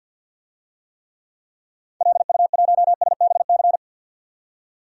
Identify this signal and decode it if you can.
Morse code, heard as a keyed tone, about 51 wpm, 700 Hz. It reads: LU1SBC
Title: DXCC-100 Call Signs - 6 Characters - 50wpm